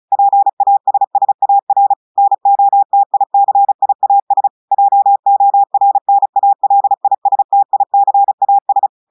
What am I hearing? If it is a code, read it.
Morse code, heard as a keyed tone, about 34 wpm, 800 Hz. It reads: PASSAR NOTICIAS JORNALISTICAS
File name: portugais texte en clair 2 vitesse 35 mots minutes